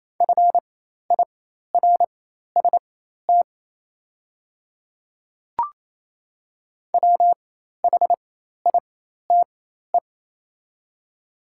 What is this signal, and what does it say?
Morse code, heard as a keyed tone, about 27 wpm, spaced as for 13 wpm, 700 Hz. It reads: FIRST WHITE
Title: 4x Character Spacing - Top 300 Words - 28wpm